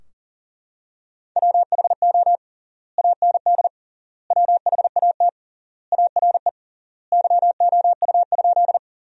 Morse code, heard as a keyed tone, about 40 wpm, 700 Hz. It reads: WHO AND WHAT ARE YOU?
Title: 3x Word-Spacing - Sentences from Top 100 Words - 40wpm